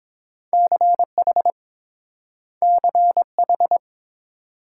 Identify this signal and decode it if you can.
Morse code, heard as a keyed tone, about 30 wpm, 700 Hz. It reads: CH CH